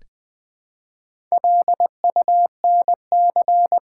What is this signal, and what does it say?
Morse code, heard as a keyed tone, about 20 wpm, 700 Hz. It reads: LUNC
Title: Build A Word - 1 Letter At A Time - 5-Letter Words - 20wpm